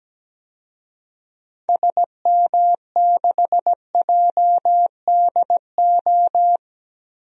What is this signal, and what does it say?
Morse code, heard as a keyed tone, about 17 wpm, 700 Hz. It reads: SM6JDO
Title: DXCC-100 Call Signs - 6 Characters - Repeated 3x - 17wpm